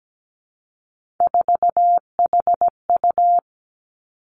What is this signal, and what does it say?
Morse code, heard as a keyed tone, about 17 wpm, 700 Hz. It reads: 4HU
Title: Three Challenging Letter-Number Combinations 17wpm